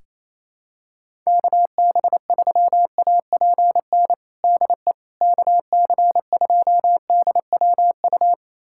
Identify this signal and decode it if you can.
Morse code, heard as a keyed tone, about 28 wpm, 700 Hz. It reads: KB3APN DE KC2DWU
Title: USA Callsign DE Callsign - 6 Characters - 28wpm